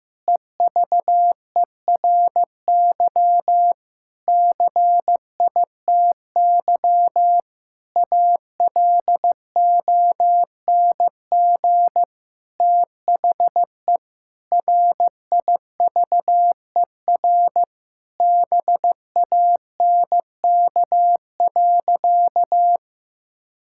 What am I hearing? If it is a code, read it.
Morse code, heard as a keyed tone, about 15 wpm, 700 Hz. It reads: EVERY CITY ALONG THE RIVER BANK.
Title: Sentences from Top 1000 Words - Review Part 6 - 15wpm